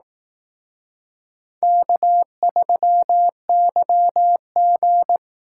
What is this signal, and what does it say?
Morse code, heard as a keyed tone, about 18 wpm, 700 Hz. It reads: K3YG
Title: Speed Racing - USA 1x2 Call Signs 15wpm